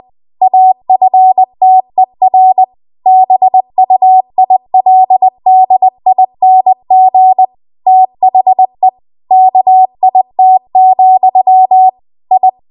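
Morse code, heard as a keyed tone, about 20 wpm, 750 Hz. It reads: AFTER BUILDING THE KIT, I
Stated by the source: Morse code practice at 20 WPM.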